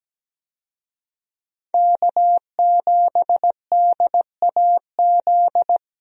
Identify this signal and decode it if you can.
Morse code, heard as a keyed tone, about 17 wpm, 700 Hz. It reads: K7DAZ
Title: USA Call Signs - 5 Characters - 17wpm